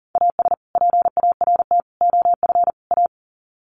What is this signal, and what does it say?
Morse code, heard as a keyed tone, about 40 wpm, 700 Hz. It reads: AS PART OF A
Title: Sets of 4 Words 40wpm